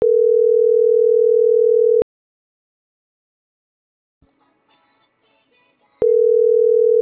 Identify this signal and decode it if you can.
Ringback tone (North America), 440 + 480 Hz.